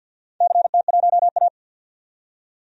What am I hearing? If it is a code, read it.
Morse code, heard as a keyed tone, about 46 wpm, 700 Hz. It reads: CT1A
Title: DXCC-100 Call Signs - 4 Characters - 50wpm